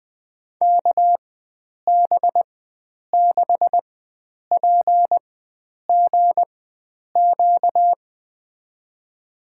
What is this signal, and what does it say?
Morse code, heard as a keyed tone, about 20 wpm, spaced as for 9 wpm, 700 Hz. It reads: KB6PGQ
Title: USA Call Signs - 6 Characters - 4x Character Spacing - 20wpm